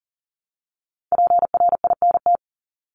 Morse code, heard as a keyed tone, about 40 wpm, 700 Hz. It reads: PRINT
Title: Top 1251-1500 Words 40wpm